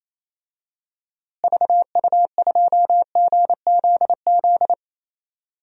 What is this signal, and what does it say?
Morse code, heard as a keyed tone, about 28 wpm, 700 Hz. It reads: VU2GZZ
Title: DXCC-100 Call Signs - 6 Characters - Repeated 3x - 28wpm